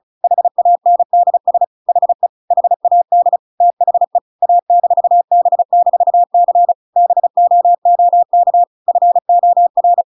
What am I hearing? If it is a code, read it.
Morse code, heard as a keyed tone, about 34 wpm, 700 Hz. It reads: HANDS HE HAD THE A-B-C BOOK FOR
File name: Pinocchio / Pinocchio ch8 35 35 ik1hge